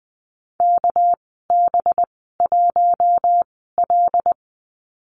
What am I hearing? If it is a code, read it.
Morse code, heard as a keyed tone, about 20 wpm, spaced as for 15 wpm, 700 Hz. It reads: KB1L
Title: 213 Lesson 34g - Number 8 - Call Signs 20wpm